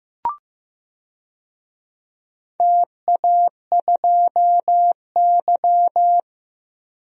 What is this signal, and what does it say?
Morse code, heard as a keyed tone, about 15 wpm, 700 Hz. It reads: TA2Y
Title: DXCC-100 Call Signs - 4 Characters - 15wpm